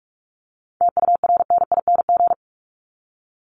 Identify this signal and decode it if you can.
Morse code, heard as a keyed tone, about 44 wpm, 700 Hz. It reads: TURNING
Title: ING Suffix - Top 100 Words with Base Words 45wpm